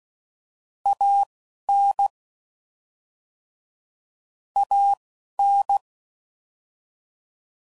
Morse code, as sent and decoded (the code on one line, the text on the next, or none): .- / -. / .- / -.
A N A N